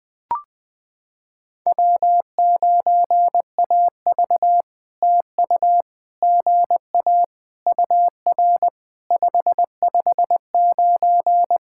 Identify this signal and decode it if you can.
Morse code, heard as a keyed tone, about 20 wpm, 700 Hz. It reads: W9AV TU GA UR 559
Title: POTA QSO Practice - 20wpm